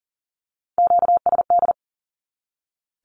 Morse code, heard as a keyed tone, about 40 wpm, 700 Hz. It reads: QSD
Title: Q-Codes 40wpm